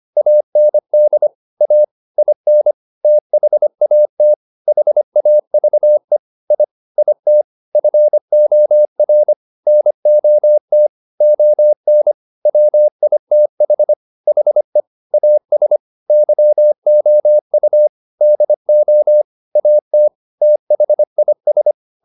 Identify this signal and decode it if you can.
Morse code, heard as a keyed tone, about 25 wpm, 600 Hz. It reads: AND A IN THAT HAVE I IT FOR NOT ON WITH HE AS YOU DO AT THIS
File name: Top-100-Words-25